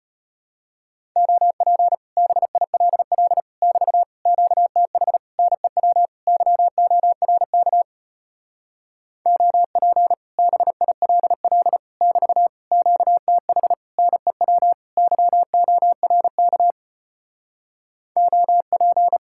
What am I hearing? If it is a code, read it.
Morse code, heard as a keyed tone, about 33 wpm, 700 Hz. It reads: OP BILL <BT> QTH NEW YORK OP BILL <BT> QTH NEW YORK OP